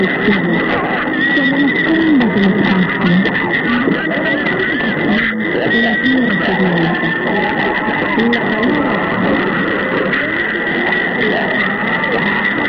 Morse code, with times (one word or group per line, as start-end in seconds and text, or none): none